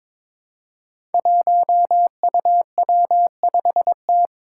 Morse code, heard as a keyed tone, about 22 wpm, 700 Hz. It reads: 1UW5T
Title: Five Challenging Letter-Number Combinations 22wpm